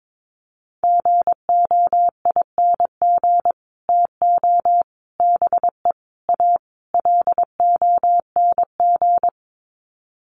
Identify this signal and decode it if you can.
Morse code, heard as a keyed tone, about 22 wpm, 700 Hz. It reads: GOING TO BE A LONG
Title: Sets of 5 Words Encore2 22wpm